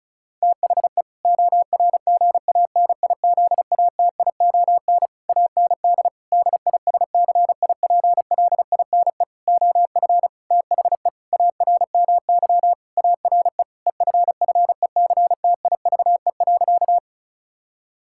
Morse code, heard as a keyed tone, about 36 wpm, 700 Hz. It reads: THE ORGANIZATION AND DISCIPLINE OF THE ARMY ARE EFFECTIVE.
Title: Sentences from Top 1000 Words - Review Part 6 - 35wpm